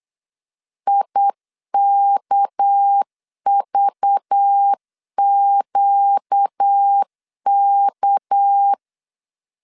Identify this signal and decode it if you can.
Morse code, heard as a keyed tone, about 8 wpm, 800 Hz. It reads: IKVQK
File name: LEZIONI / 6 PKX0 / 6 PK 8wpm 3003